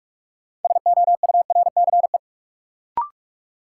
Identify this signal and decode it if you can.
Morse code, heard as a keyed tone, about 42 wpm, 700 Hz. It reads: SOURCE
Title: Sentences from Top 800 Words - New Word Drill - 45wpm